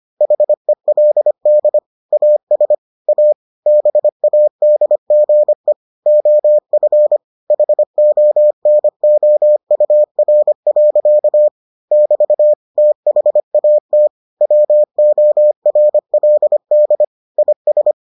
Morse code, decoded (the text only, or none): HELD AS A BADGE OF HONOUR. <BT> THAT WORLD IS